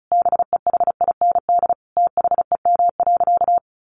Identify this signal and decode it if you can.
Morse code, heard as a keyed tone, about 34 wpm, 700 Hz. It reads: BEHIND THEM.
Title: Sentences from Top 1300 Words 35wpm